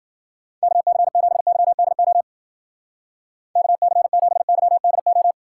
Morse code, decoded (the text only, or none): KC7YDO KC7YDO